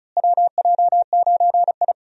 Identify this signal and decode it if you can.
Morse code, heard as a keyed tone, about 36 wpm, 700 Hz. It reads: WJ9I
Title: USA Call Signs - 4 Characters - Repeated 3x - 35wpm